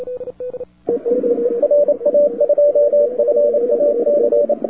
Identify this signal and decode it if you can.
Morse code, heard as a keyed tone, about 27 wpm, 580 Hz, with radio noise. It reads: RA2FAC
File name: wwcw08 / k5zd wwcw08 1400z